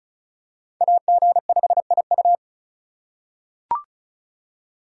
Morse code, heard as a keyed tone, about 36 wpm, 700 Hz. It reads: AG5IU
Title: USA Call Signs - 4x Character Spacing - 35wpm